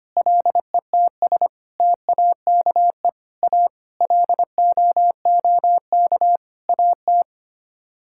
Morse code, heard as a keyed tone, about 25 wpm, 700 Hz. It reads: LETS TAKE A LOOK AT
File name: Sets of 9 Words - 25wpm